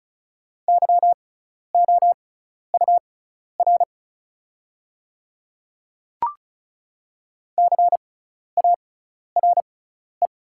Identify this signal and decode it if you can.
Morse code, heard as a keyed tone, about 36 wpm, spaced as for 12 wpm, 690 Hz. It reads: YOUR CARE
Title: 6x Character Spacing - Top 300 Words - 35wpm